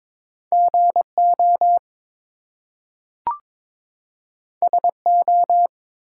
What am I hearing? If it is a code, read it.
Morse code, heard as a keyed tone, about 22 wpm, 700 Hz. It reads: GO SO
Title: Two Letter Words - 22wpm